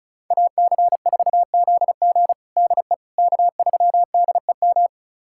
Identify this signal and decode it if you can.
Morse code, heard as a keyed tone, about 36 wpm, 700 Hz. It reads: AC4ZG DE K3DEM
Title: USA Callsign DE Callsign - 5 Characters - 35wpm